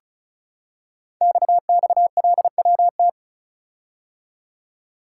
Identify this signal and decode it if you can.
Morse code, heard as a keyed tone, about 36 wpm, 700 Hz. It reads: XXLWT
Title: Code Groups - 5 Letters - 35wpm